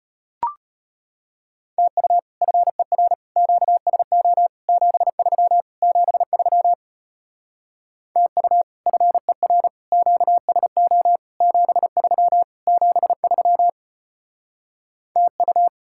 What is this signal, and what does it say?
Morse code, heard as a keyed tone, about 33 wpm, 700 Hz. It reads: TU FER QSO 73 73 TU FER QSO 73 73 TU